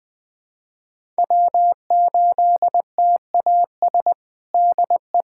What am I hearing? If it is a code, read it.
Morse code, heard as a keyed tone, about 19 wpm, 700 Hz. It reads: W8TAS DE
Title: USA Callsign DE Callsign 20wpm